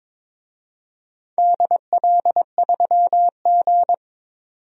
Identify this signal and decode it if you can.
Morse code, heard as a keyed tone, about 22 wpm, 700 Hz. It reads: DL3G
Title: DXCC-100 Call Signs - 4 Characters - 22wpm